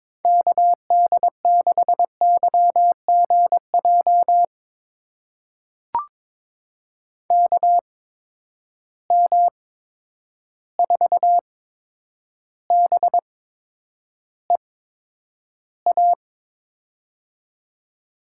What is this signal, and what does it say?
Morse code, heard as a keyed tone, about 22 wpm, spaced as for 9 wpm, 700 Hz. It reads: KD6YGJ KM4BEA
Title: USA Call Signs - 6 Characters - 8x Character Spacing - 22wpm